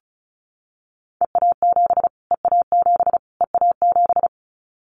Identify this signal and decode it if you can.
Morse code, heard as a keyed tone, about 34 wpm, 700 Hz. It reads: EA7 EA7 EA7
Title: Callsign Prefixes for Top 25 DXCC - Repeated 3x - 35wpm